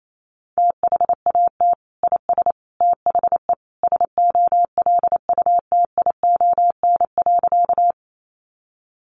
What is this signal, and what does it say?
Morse code, heard as a keyed tone, about 28 wpm, 700 Hz. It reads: THAT IS THE SOLUTION.
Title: Sentences from Top 1300 Words 28wpm